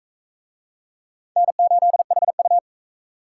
Morse code, heard as a keyed tone, about 39 wpm, 700 Hz. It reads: N8HU